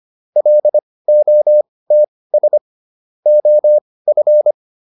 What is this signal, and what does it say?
Morse code, heard as a keyed tone, about 25 wpm, spaced as for 18 wpm, 600 Hz. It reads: LOTS OF